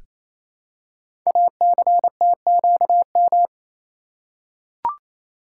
Morse code, heard as a keyed tone, about 28 wpm, 700 Hz. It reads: ACTQM
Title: Code Groups - 5 Letters - 28wpm